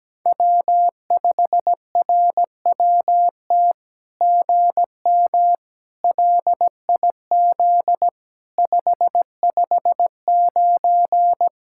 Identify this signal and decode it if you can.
Morse code, heard as a keyed tone, about 17 wpm, 700 Hz. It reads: W5RWT GM LIZ 559
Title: SOTA QSO Practice 17wpm